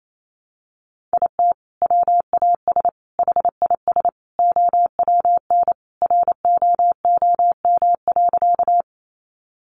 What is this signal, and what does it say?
Morse code, heard as a keyed tone, about 28 wpm, 700 Hz. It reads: IT WAS HIS OWN ROOM.
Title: Sentences from Top 300 Words 28wpm